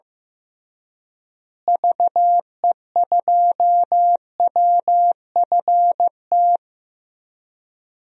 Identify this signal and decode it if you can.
Morse code, heard as a keyed tone, about 15 wpm, 700 Hz. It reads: VE2WFT
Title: DXCC-100 Call Signs - 6 Characters - Repeated 3x - 15wpm